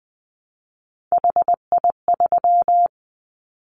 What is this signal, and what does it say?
Morse code, heard as a keyed tone, about 20 wpm, 700 Hz. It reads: HI3
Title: Callsign Prefixes for Top 100 DXCC - 20wpm